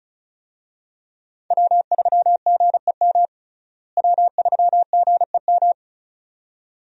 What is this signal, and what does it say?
Morse code, heard as a keyed tone, about 34 wpm, 700 Hz. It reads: W3GEM W3GEM
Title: USA Call Signs - 5 Characters - Repeated 2x - 35wpm